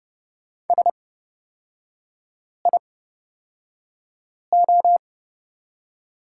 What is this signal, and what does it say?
Morse code, heard as a keyed tone, about 29 wpm, spaced as for 5 wpm, 700 Hz. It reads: SIO
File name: Lesson 38e - Letter q - Three Characters 30wpm